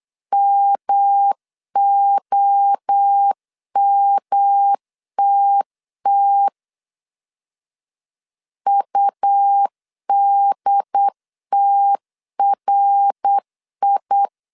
Morse code, heard as a keyed tone, about 8 wpm, 800 Hz. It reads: MOMTT UDTRI